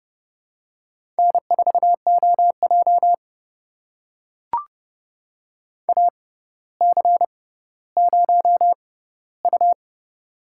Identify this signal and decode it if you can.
Morse code, heard as a keyed tone, about 30 wpm, spaced as for 15 wpm, 700 Hz. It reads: N4OJ AC0U
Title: USA Call Signs - 4 Characters - 6x Character Spacing - 30wpm